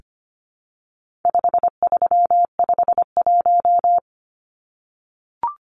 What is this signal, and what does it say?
Morse code, heard as a keyed tone, about 25 wpm, 700 Hz. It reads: 5351
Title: Numbers - 4-Digit 25wpm